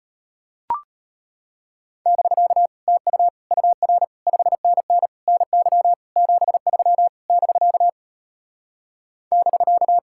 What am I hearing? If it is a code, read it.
Morse code, heard as a keyed tone, about 33 wpm, 700 Hz. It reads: <BK> TU UR 5NN NY 73 <BK> <BK>